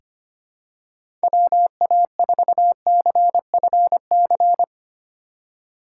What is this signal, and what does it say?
Morse code, heard as a keyed tone, about 25 wpm, 700 Hz. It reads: WA4CFC
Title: USA Call Signs - 6 Characters - 8x Character Spacing - 25wpm